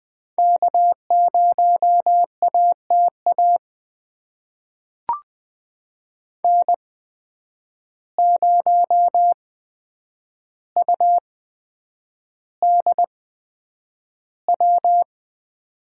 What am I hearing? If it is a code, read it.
Morse code, heard as a keyed tone, about 20 wpm, spaced as for 9 wpm, 700 Hz. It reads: K0ATA N0UDW
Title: USA Call Signs - 5 Characters - 8x Character Spacing - 20wpm